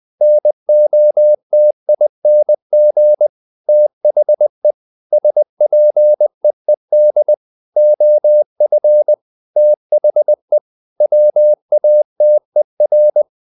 Morse code, heard as a keyed tone, about 20 wpm, 600 Hz. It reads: NOTING THE SPEED OF THE WATER
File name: the old man and the sea part 2 20wpm